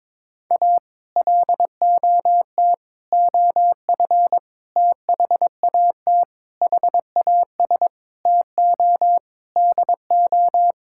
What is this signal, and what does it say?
Morse code, heard as a keyed tone, about 22 wpm, 700 Hz. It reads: A LOT OF THAT HAS TO DO
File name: practice / 22wpm / Sets of 8 Words - 22wpm